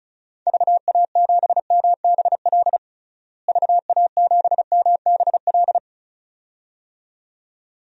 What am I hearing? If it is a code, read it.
Morse code, heard as a keyed tone, about 36 wpm, 700 Hz. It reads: VA7MBL VA7MBL
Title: DXCC-100 Call Signs - 6 Characters - Repeated 2x - 35wpm